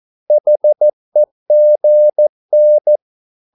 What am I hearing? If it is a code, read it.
Morse code, heard as a keyed tone, about 14 wpm, 600 Hz. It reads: HEGN
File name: Korte ord DK 14wpm